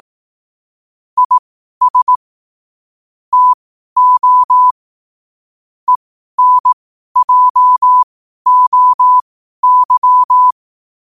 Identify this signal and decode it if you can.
Morse code, heard as a keyed tone, about 18 wpm, spaced as for 12 wpm, 1000 Hz. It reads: IS TO ENJOY
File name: qotdfiles / 12wpm